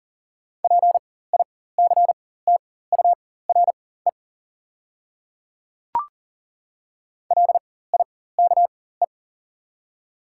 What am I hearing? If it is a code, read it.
Morse code, heard as a keyed tone, about 40 wpm, spaced as for 19 wpm, 690 Hz. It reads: PICTURE LIKE
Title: 4x Character Spacing - Top 200 Words - 40wpm